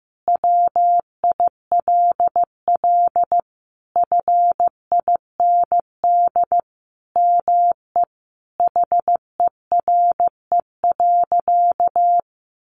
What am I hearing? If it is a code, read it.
Morse code, heard as a keyed tone, about 15 wpm, 700 Hz. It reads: WILL FIND ME HERE.
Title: Sentences from Top 1200 Words 15wpm